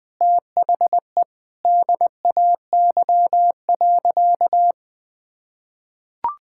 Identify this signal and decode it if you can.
Morse code, heard as a keyed tone, about 20 wpm, 700 Hz. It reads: THE DAY.
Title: Sentences with 2-Word Phrases - 20wpm